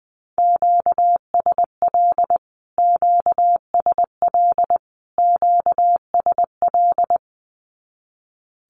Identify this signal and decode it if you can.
Morse code, heard as a keyed tone, about 20 wpm, 700 Hz. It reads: QSL QSL QSL
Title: Q-Codes Repeated 3x 20wpm